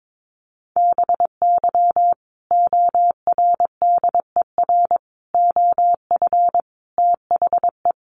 Morse code, heard as a keyed tone, about 22 wpm, 700 Hz. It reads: BY ORDER OF THE
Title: Sentences from Top 1300 Words 22wpm